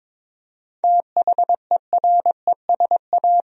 Morse code, heard as a keyed tone, about 22 wpm, 700 Hz. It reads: THERESA
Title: USA Common Names - 22wpm